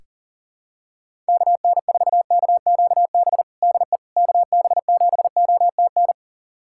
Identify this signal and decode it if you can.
Morse code, heard as a keyed tone, about 40 wpm, 700 Hz. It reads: KN4KQB DE KB7OTN
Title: USA Callsign DE Callsign 40wpm